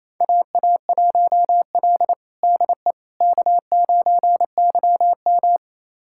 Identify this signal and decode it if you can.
Morse code, heard as a keyed tone, about 27 wpm, 700 Hz. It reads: AA1L DE K9YM
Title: USA Callsign DE Callsign - 4 Characters - 28wpm